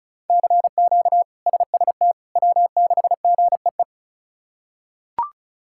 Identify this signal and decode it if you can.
Morse code, heard as a keyed tone, about 34 wpm, 700 Hz. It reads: CQ SST W6GEE
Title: K1USN SST QSO Practice - 35wpm